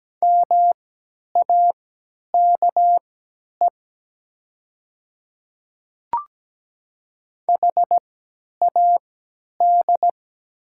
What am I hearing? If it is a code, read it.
Morse code, heard as a keyed tone, about 17 wpm, spaced as for 10 wpm, 700 Hz. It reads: MAKE HAD
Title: 3x Character Spacing - Top 100 Words - 17wpm